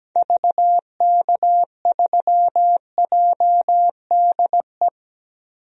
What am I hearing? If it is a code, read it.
Morse code, heard as a keyed tone, about 17 wpm, 700 Hz. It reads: VK3JDE
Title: DXCC-100 Call Signs - 6 Characters - 17wpm